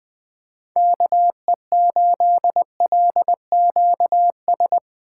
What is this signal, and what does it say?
Morse code, heard as a keyed tone, about 20 wpm, 700 Hz. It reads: KE8LQS